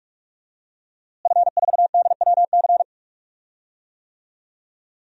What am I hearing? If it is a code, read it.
Morse code, heard as a keyed tone, about 46 wpm, 700 Hz. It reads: F4DWC
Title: DXCC-100 Call Signs - 5 Characters - Repeated 2x - 45wpm